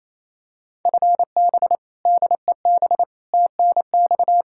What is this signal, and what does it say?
Morse code, heard as a keyed tone, about 27 wpm, 700 Hz. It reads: FB DEB TNX
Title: Practice QSO - Exchange 3 - More Info - 28wpm